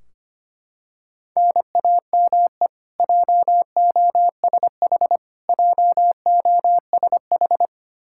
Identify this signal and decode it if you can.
Morse code, heard as a keyed tone, about 25 wpm, 700 Hz. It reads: NAME JOSH JOSH
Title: Practice QSO - Exchange 1 - Beginning - 25wpm